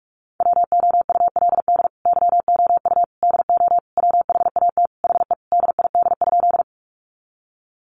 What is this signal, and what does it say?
Morse code, heard as a keyed tone, about 44 wpm, 700 Hz. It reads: WOULD YOU DO WHAT HE DID?
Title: Sentences from Top 100 Words 45wpm